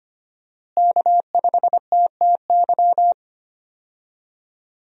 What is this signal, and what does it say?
Morse code, heard as a keyed tone, about 25 wpm, 700 Hz. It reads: K5TTY
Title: USA Call Signs - 5 Characters - Repeated 3x - 25wpm